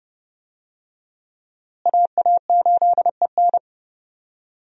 Morse code, heard as a keyed tone, about 30 wpm, 700 Hz. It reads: AA8EN
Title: USA Call Signs - 5 Characters - Repeated 2x - 30wpm